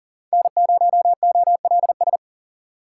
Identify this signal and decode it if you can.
Morse code, heard as a keyed tone, about 38 wpm, 700 Hz. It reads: N0OLS
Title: USA Call Signs - 5 Characters - 2x Character Spacing - 40wpm